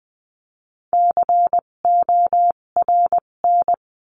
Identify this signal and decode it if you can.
Morse code, heard as a keyed tone, about 20 wpm, spaced as for 17 wpm, 700 Hz. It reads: CORN
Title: 257 Lesson 42b - Words 14-20wpm